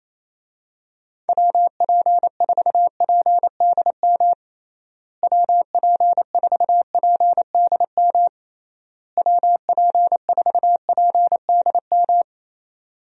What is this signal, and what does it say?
Morse code, heard as a keyed tone, about 28 wpm, 700 Hz. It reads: WP4PDM WP4PDM WP4PDM
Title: USA Call Signs - 6 Characters - Repeated 3x - 28wpm